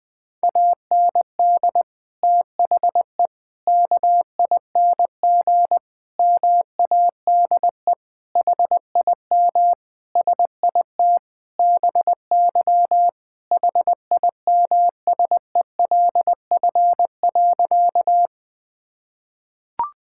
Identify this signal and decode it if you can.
Morse code, heard as a keyed tone, about 21 wpm, 700 Hz. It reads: AND THE KING MADE HIM SIT BY HIMSELF.
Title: Sentences from Top 1600 Words 20wpm